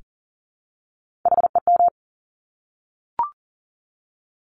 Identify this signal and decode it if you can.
Morse code, heard as a keyed tone, about 39 wpm, 700 Hz. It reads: HEM
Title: Top Three Letter Combinations in Words 40wpm